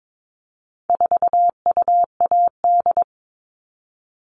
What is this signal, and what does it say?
Morse code, heard as a keyed tone, about 22 wpm, 700 Hz. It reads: 4UAD
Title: All Cut Numbers - 4-Digit 22wpm